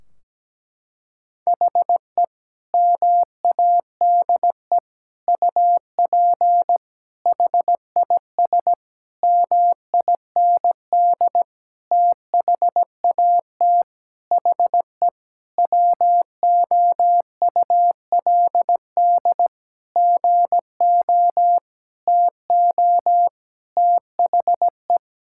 Morse code, heard as a keyed tone, about 17 wpm, 700 Hz. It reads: HE MADE UP HIS MIND THAT HE WOULD GO TO THE
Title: Sentences from Top 1600 Words 17wpm